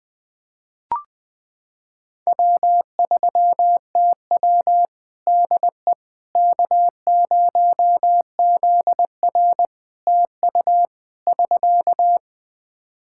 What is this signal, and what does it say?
Morse code, heard as a keyed tone, about 21 wpm, 700 Hz. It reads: W3TW DE K0ZR TU <SK>
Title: K1USN SST QSO Practice 20wpm